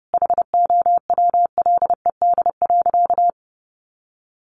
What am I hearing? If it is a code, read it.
Morse code, heard as a keyed tone, about 28 wpm, 700 Hz. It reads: HOWLED.
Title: A Sound Of Thunder 30wpm